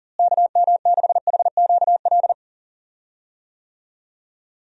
Morse code, heard as a keyed tone, about 37 wpm, 700 Hz. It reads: KM6HQL
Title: USA Call Signs - 6 Characters - Repeated 3x - 40wpm